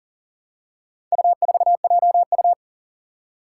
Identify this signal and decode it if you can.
Morse code, heard as a keyed tone, about 40 wpm, 700 Hz. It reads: U4JU
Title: Four Challenging Letter-Number Combinations 40wpm